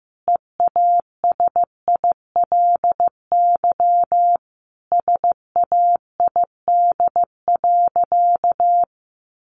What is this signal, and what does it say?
Morse code, heard as a keyed tone, about 15 wpm, 700 Hz. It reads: EASILY SAID.
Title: Sentences from Top 900 Words 15wpm